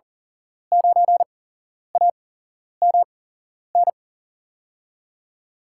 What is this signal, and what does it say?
Morse code, heard as a keyed tone, about 38 wpm, spaced as for 11 wpm, 700 Hz. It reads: 9AMN
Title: USA Call Signs - 8x Character Spacing - 40wpm